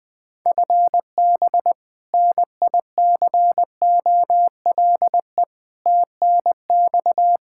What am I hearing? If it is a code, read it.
Morse code, heard as a keyed tone, about 20 wpm, 700 Hz. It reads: FB NICOLE TNX
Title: Practice QSO - Exchange 3 - More Info - 20wpm